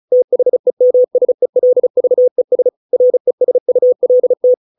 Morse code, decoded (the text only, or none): THEMSELVES RESULT